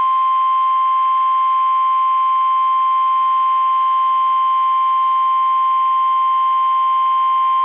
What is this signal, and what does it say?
Emergency Alert System (SAME) tones (the 1050 Hz attention tone). It is heard from 0.5 to 7.4 s.